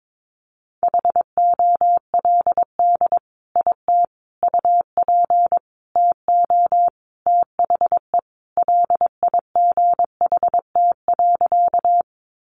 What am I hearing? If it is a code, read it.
Morse code, heard as a keyed tone, about 22 wpm, 700 Hz. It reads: HOLD IT UP TO THE LIGHT.
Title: Sentences from Top 400 Words 22wpm